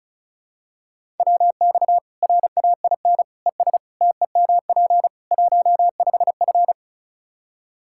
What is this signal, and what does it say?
Morse code, heard as a keyed tone, about 34 wpm, 700 Hz. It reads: WX RAIN ES TEMP 15F
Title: Practice QSO - Exchange 2 - Info - 35wpm